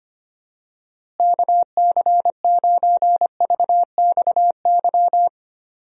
Morse code, heard as a keyed tone, about 25 wpm, 700 Hz. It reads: KC9VXY
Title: USA Call Signs - 6 Characters - Repeated 2x - 25wpm